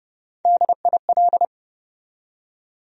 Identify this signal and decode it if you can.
Morse code, heard as a keyed tone, about 30 wpm, 700 Hz. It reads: DIL
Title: USA Call Sign Postfixes - 3 Letters 30wpm